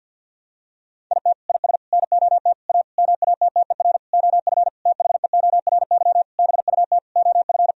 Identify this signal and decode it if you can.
Morse code, heard as a keyed tone, about 52 wpm, 700 Hz. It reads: IT IS NOT A MATTER OF THEORY BUT OF
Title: Sentences from Top 1000 Words - Review Part 4 - 50wpm